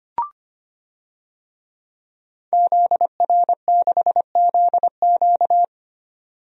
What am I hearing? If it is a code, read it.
Morse code, heard as a keyed tone, about 25 wpm, 700 Hz. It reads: ZR6ZQ
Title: DXCC-100 Call Signs - 5 Characters - 25wpm